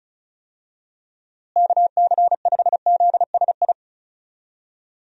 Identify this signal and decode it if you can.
Morse code, heard as a keyed tone, about 36 wpm, 700 Hz. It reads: KC5ZSI
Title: USA Call Signs - 6 Characters - Repeated 2x - 35wpm